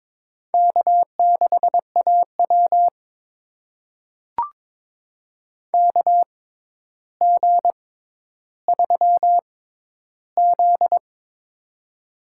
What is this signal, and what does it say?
Morse code, heard as a keyed tone, about 22 wpm, spaced as for 11 wpm, 700 Hz. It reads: K6AW KG3Z
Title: USA Call Signs - 6x Character Spacing - 22wpm